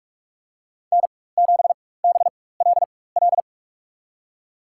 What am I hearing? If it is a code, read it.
Morse code, heard as a keyed tone, about 46 wpm, spaced as for 21 wpm, 700 Hz. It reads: N7BLL
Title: USA Call Signs - 5 Characters - 4x Character Spacing - 45wpm